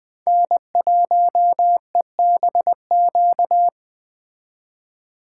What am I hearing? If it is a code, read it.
Morse code, heard as a keyed tone, about 19 wpm, 700 Hz. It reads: N1EBQ